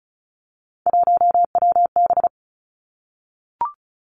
Morse code, heard as a keyed tone, about 36 wpm, 700 Hz. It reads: 1WB
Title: Three Challenging Letter-Number Combinations 35wpm